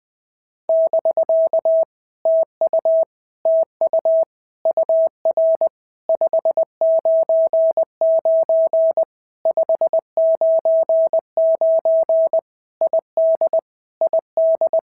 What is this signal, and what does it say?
Morse code, heard as a keyed tone, about 20 wpm, 650 Hz. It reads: <BK> TU TU UR 599 599 ID ID
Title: POTA QSO Practice - 20wpm